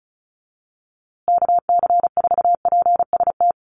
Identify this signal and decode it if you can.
Morse code, heard as a keyed tone, about 34 wpm, 700 Hz. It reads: KC4PST
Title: USA Call Signs Repeated 3x 35wpm